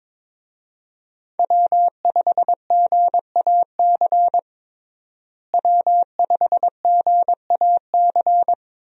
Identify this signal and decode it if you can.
Morse code, heard as a keyed tone, about 21 wpm, 700 Hz. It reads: W5GAC W5GAC
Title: USA Call Signs - 5 Characters - Repeated 2x - 22wpm